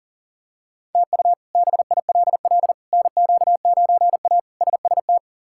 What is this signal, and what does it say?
Morse code, heard as a keyed tone, about 40 wpm, 700 Hz. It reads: TU BILL NQ9A SST
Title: K1USN SST QSO Practice - 40wpm